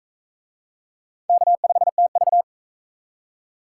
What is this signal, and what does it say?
Morse code, heard as a keyed tone, about 37 wpm, 700 Hz. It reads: K5TV